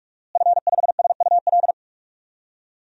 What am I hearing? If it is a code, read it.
Morse code, heard as a keyed tone, about 46 wpm, 700 Hz. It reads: F5SUL
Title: DXCC-100 Call Signs - 5 Characters - Repeated 3x - 45wpm